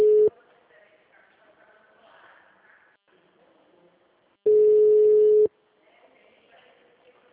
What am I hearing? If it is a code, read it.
Ringback tone (Europe, 425 Hz).
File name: out- -2002- - - .24